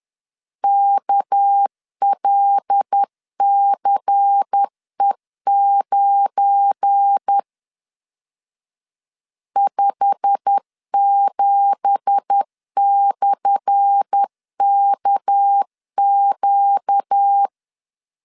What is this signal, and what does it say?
Morse code, heard as a keyed tone, about 10 wpm, 800 Hz. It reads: KLCE9 57/KQ
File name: LEZIONI / 10 SIGNS / 10A SLASH 10wpm 2808